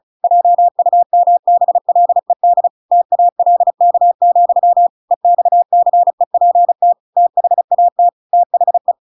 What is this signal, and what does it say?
Morse code, heard as a keyed tone, about 34 wpm, 700 Hz. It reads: JUMBLED TALK, EXCEPT THAT THE
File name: Pinocchio / Pinocchio ch7 35 35 ik1hge